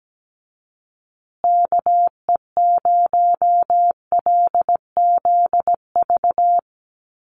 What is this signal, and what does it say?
Morse code, heard as a keyed tone, about 17 wpm, 700 Hz. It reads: KE0LZV
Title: USA Call Signs Repeated 2x 17wpm